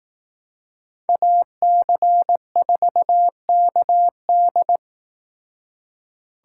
Morse code, heard as a keyed tone, about 18 wpm, 700 Hz. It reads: AC4KD
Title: Speed Racing - USA 2x2 Call Signs 15wpm